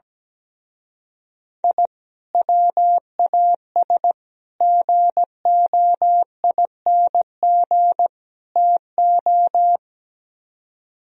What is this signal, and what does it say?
Morse code, heard as a keyed tone, about 17 wpm, 700 Hz. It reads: I WAS GOING TO
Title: Speed-Racing - Sets of 4 Words 17wpm